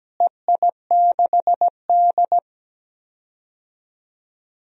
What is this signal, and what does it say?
Morse code, heard as a keyed tone, about 17 wpm, 700 Hz. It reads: EI6D
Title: DXCC-100 Call Signs - 4 Characters - Repeated 2x - 17wpm